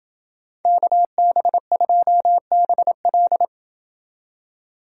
Morse code, heard as a keyed tone, about 27 wpm, 700 Hz. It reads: KB2BL